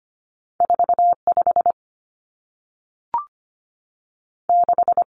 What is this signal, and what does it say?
Morse code, heard as a keyed tone, about 25 wpm, 700 Hz. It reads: 4 5 6
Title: Common Cut Numbers - 2-Digit 25wpm